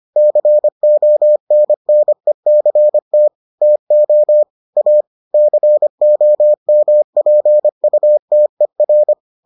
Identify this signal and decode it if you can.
Morse code, heard as a keyed tone, about 25 wpm, 600 Hz. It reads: CONNECT TO A COMPUTER